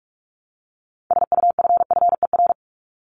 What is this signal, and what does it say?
Morse code, heard as a keyed tone, about 44 wpm, 700 Hz. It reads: SUFFER
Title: ING Suffix - Top 100 Words with Base Words 45wpm